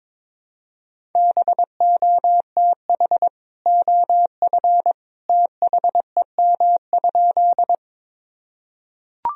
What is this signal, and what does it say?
Morse code, heard as a keyed tone, about 22 wpm, 700 Hz. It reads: BOTH OF THEM?
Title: Sentences from Top 1000 Words - Review Part 1 - 22wpm